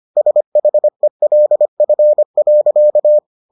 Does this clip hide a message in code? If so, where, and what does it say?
Yes: Morse code from 0.2 to 3.2 s, heard as a keyed tone, about 25 wpm, 600 Hz. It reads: SHELF.